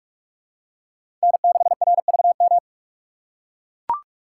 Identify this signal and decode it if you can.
Morse code, heard as a keyed tone, about 44 wpm, 700 Hz. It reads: N6RVM
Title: USA Call Signs - 5 Characters - 6x Character Spacing - 45wpm